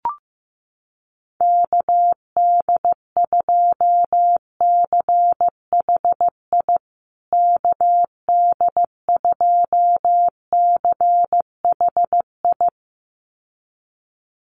Morse code, heard as a keyed tone, about 15 wpm, 700 Hz. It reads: KD2CHI KD2CHI
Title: USA Call Signs Repeated 2x 15wpm